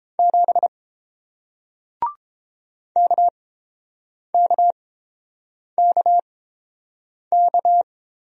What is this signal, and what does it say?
Morse code, heard as a keyed tone, about 30 wpm, spaced as for 7 wpm, 700 Hz. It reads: 7 KKKK